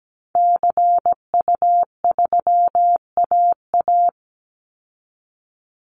Morse code, heard as a keyed tone, about 17 wpm, 700 Hz. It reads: CU3AA
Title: Call Signs Repeated 3x - 17wpm